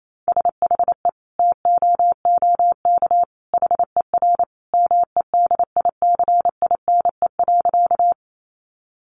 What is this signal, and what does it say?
Morse code, heard as a keyed tone, about 28 wpm, 700 Hz. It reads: SHE TOOK HER MEDICINE.
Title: Sentences from Top 900 Words 28wpm